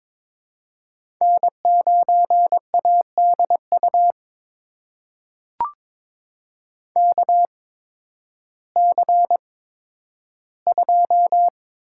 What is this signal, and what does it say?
Morse code, heard as a keyed tone, about 22 wpm, spaced as for 12 wpm, 700 Hz. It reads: N9ADU KC2
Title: USA Call Signs - 8x Character Spacing - 22wpm